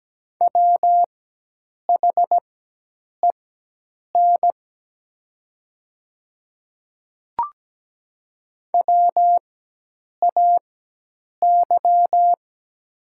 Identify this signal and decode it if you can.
Morse code, heard as a keyed tone, about 17 wpm, spaced as for 8 wpm, 690 Hz. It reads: WHEN WAY
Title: 4x Character Spacing - Top 100 Words - 17wpm